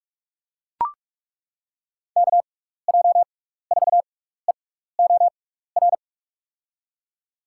Morse code, heard as a keyed tone, about 43 wpm, spaced as for 15 wpm, 700 Hz. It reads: KJ4EOR
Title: USA Call Signs - 6 Characters - 6x Character Spacing - 45wpm